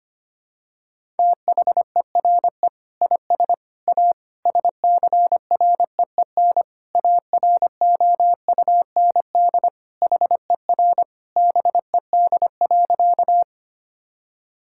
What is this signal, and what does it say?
Morse code, heard as a keyed tone, about 25 wpm, 700 Hz. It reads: THERE IS A SCREEN AROUND HER BED.
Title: Sentences from Top 1000 Words - Review Part 6 - 25wpm